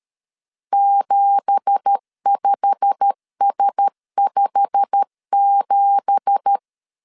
Morse code, heard as a keyed tone, about 13 wpm, 800 Hz. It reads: 75S57
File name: LEZIONI / 9 2867 12wpm 2322